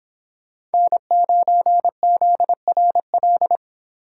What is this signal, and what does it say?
Morse code, heard as a keyed tone, about 30 wpm, 700 Hz. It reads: N9ZRL